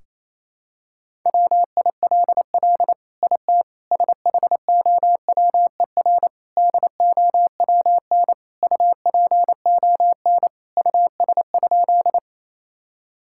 Morse code, heard as a keyed tone, about 28 wpm, 700 Hz. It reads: WILL IT SHOWER DOWN UPON US?
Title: Sentences from Top 1600 Words 28wpm